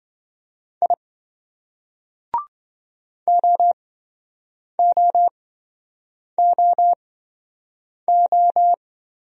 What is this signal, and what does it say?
Morse code, heard as a keyed tone, about 26 wpm, spaced as for 7 wpm, 700 Hz. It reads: I OOOO